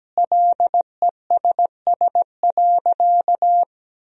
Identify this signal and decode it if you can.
Morse code, heard as a keyed tone, about 17 wpm, 700 Hz. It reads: LESS.
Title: Sentences from Top 1000 Words - Review Part 2 - 17wpm